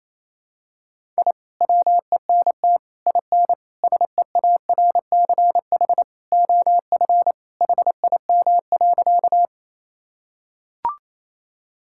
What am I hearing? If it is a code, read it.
Morse code, heard as a keyed tone, about 28 wpm, 700 Hz. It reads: I WENT IN SEARCH OF HIM.
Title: Sentences from Top 1000 Words - Review Part 5 - 28wpm